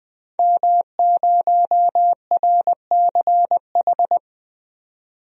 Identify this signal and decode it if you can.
Morse code, heard as a keyed tone, about 20 wpm, 700 Hz. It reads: M0RCH
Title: DXCC-100 Call Signs - 5 Characters - 20wpm